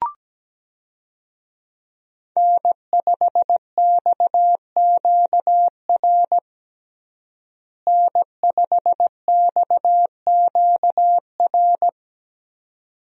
Morse code, heard as a keyed tone, about 17 wpm, 700 Hz. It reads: N5XQR N5XQR
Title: USA Call Signs - 5 Characters - Repeated 2x - 17wpm